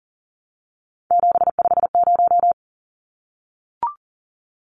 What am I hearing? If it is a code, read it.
Morse code, heard as a keyed tone, about 40 wpm, 700 Hz. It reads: 750
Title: Numbers - 3-Digit 40wpm